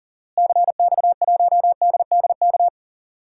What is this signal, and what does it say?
Morse code, heard as a keyed tone, about 40 wpm, 700 Hz. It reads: CX1DDK
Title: DXCC-100 Call Signs - 6 Characters - Repeated 3x - 40wpm